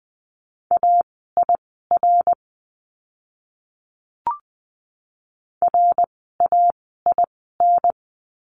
Morse code, heard as a keyed tone, about 20 wpm, spaced as for 15 wpm, 700 Hz. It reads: AIR RAIN
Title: 054 Lesson 10f - Letter r - Words 20wpm